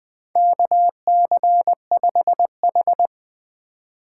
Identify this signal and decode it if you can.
Morse code, heard as a keyed tone, about 20 wpm, 700 Hz. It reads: KC5H
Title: 2023 POTA Activator Call Signs - 20wpm